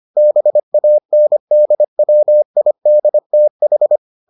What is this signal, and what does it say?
Morse code, heard as a keyed tone, about 25 wpm, 600 Hz. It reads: BANDWIDTH